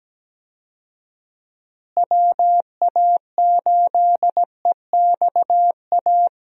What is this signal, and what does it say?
Morse code, heard as a keyed tone, about 17 wpm, 700 Hz. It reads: WA8EXA
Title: USA Call Signs - 6 Characters - Repeated 2x - 17wpm